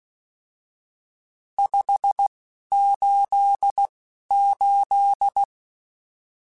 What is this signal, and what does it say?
Morse code, heard as a keyed tone, about 16 wpm, spaced as for 11 wpm, 800 Hz. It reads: 588
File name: RSTs / RST-mp3s / Various RST @10